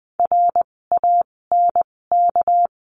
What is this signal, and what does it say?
Morse code, heard as a keyed tone, about 20 wpm, spaced as for 16 wpm, 700 Hz. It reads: RANK
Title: 256 Lesson 42a - Words 12-20wpm